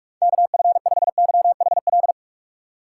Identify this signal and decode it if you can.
Morse code, heard as a keyed tone, about 46 wpm, 700 Hz. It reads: KF5YHL
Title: USA Call Signs - 6 Characters - 2x Character Spacing - 45wpm